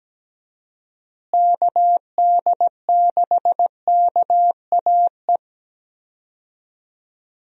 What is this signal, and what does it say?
Morse code, heard as a keyed tone, about 17 wpm, 700 Hz. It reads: KD6KAE
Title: USA Call Signs - 6 Characters - Repeated 2x - 17wpm